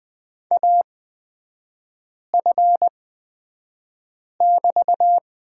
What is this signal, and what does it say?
Morse code, heard as a keyed tone, about 20 wpm, spaced as for 5 wpm, 700 Hz. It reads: AF<BT>
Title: 243 Lesson 39e - Letter z - Three Characters 20wpm